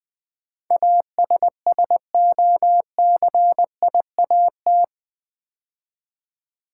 Morse code, heard as a keyed tone, about 20 wpm, 700 Hz. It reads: ASSOCIAT
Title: Build A Word - 4 Letters At A Time - 10-Letter Words - 20wpm